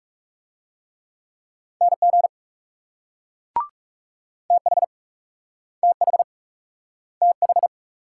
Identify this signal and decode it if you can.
Morse code, heard as a keyed tone, about 36 wpm, 700 Hz. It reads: NG TH TH TH
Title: Speed-Racing Multi-letter Phonemes - 30wpm